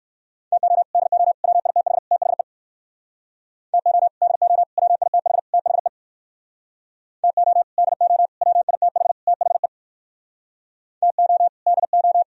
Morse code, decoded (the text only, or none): TO DO WITH THE TO DO WITH THE TO DO WITH THE TO DO